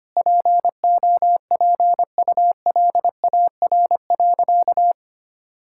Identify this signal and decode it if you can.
Morse code, heard as a keyed tone, about 25 wpm, 700 Hz. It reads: POPULAR.
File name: practice / 25wpm / Sentences with 3-Word Phrases - 25wpm